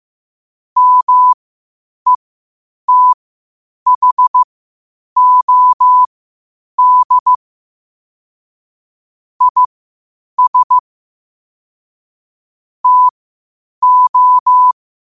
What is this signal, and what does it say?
Morse code, heard as a keyed tone, about 15 wpm, spaced as for 8 wpm, 1000 Hz. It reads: METHOD IS TO
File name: qotdfiles / 8wpm